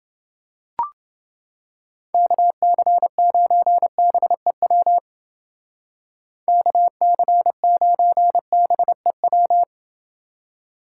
Morse code, heard as a keyed tone, about 29 wpm, 700 Hz. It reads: KC9BEW KC9BEW